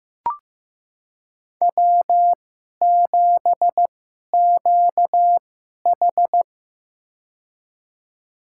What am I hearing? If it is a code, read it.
Morse code, heard as a keyed tone, about 15 wpm, spaced as for 11 wpm, 700 Hz. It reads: W7QH
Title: USA Call Signs - 4 Characters - 2x Character Spacing - 15wpm